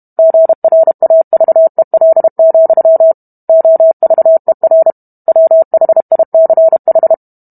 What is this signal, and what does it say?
Morse code, heard as a keyed tone, about 32 wpm, 650 Hz. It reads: GRAVEL, OVER WHICH